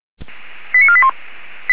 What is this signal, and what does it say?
Five-tone selective call (ZVEI), heard as quick runs of five tones, tones of about 70 ms. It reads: 89481